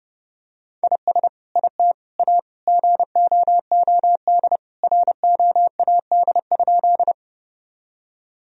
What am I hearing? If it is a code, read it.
Morse code, heard as a keyed tone, about 32 wpm, 700 Hz. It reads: IS IT A GOOD ROAD?
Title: Sentences from Top 1000 Words - Review Part 2 - 30wpm